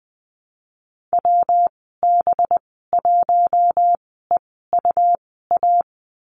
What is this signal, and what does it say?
Morse code, heard as a keyed tone, about 20 wpm, spaced as for 15 wpm, 700 Hz. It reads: WB1EUA
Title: 186 Lesson 30g - Letter b - Call Signs 20wpm